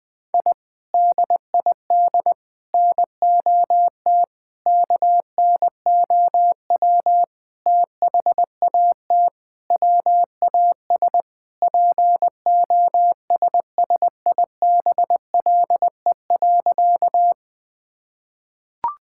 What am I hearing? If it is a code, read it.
Morse code, heard as a keyed tone, about 20 wpm, 700 Hz. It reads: I DID NOT KNOW THAT WAS POSSIBLE.
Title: Sentences with 5-Word Phrases - 20wpm